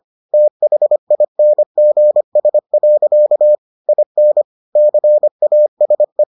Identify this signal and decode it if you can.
Morse code, heard as a keyed tone, about 25 wpm, 600 Hz. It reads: THINGS. IN CASE